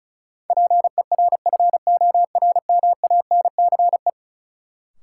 Morse code, heard as a keyed tone, about 36 wpm, 700 Hz. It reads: PERFORMANCE
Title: Sentences from Top 1000 Words - New Word Drill - 35wpm